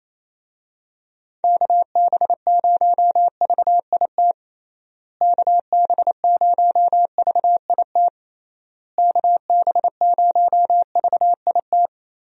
Morse code, heard as a keyed tone, about 28 wpm, 700 Hz. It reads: KB0VIT KB0VIT KB0VIT
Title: USA Call Signs - 6 Characters - Repeated 3x - 28wpm